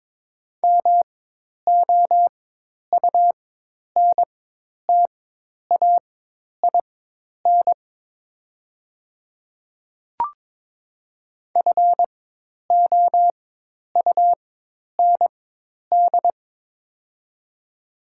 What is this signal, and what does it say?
Morse code, heard as a keyed tone, about 22 wpm, spaced as for 10 wpm, 700 Hz. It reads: MOUNTAIN FOUND
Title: 4x Character Spacing - Top 500 Words - 22wpm